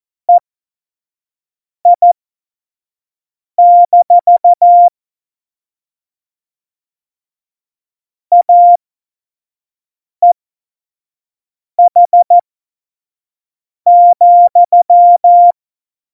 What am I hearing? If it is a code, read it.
Morse code, heard as a keyed tone, about 14 wpm, spaced as for 5 wpm, 700 Hz. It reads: EI- AEH,
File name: Morsekurs / Witze / Witz 05. Januar 2021, 25 F70